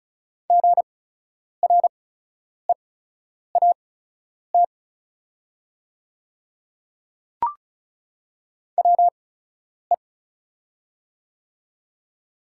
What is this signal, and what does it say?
Morse code, heard as a keyed tone, about 36 wpm, spaced as for 10 wpm, 690 Hz. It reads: GREAT WE
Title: 8x Character Spacing - Top 200 Words - 35wpm